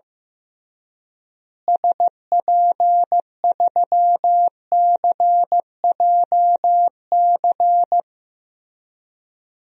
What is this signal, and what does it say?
Morse code, heard as a keyed tone, about 15 wpm, 700 Hz. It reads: SP3CJC
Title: DXCC-100 Call Signs - 6 Characters - Repeated 3x - 15wpm